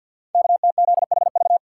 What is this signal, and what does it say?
Morse code, heard as a keyed tone, about 52 wpm, 700 Hz. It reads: KTZHV
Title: Code Groups - 5 Letters - 50wpm